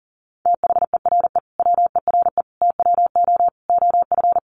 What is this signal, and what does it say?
Morse code, heard as a keyed tone, about 40 wpm, 700 Hz. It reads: THERE WERE TWO OF
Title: Sentences from Top 100 Words 40wpm